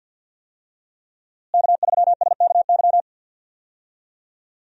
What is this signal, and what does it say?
Morse code, heard as a keyed tone, about 52 wpm, 700 Hz. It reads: K3SKY
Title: USA Call Signs - 5 Characters - Repeated 2x - 50wpm